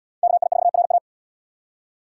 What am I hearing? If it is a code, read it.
Morse code, heard as a keyed tone, about 77 wpm, 690 Hz. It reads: KI4RA